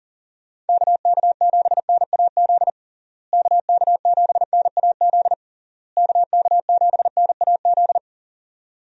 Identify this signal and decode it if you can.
Morse code, heard as a keyed tone, about 37 wpm, 700 Hz. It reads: KK7NAZ KK7NAZ KK7NAZ
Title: USA Call Signs - 6 Characters - Repeated 3x - 40wpm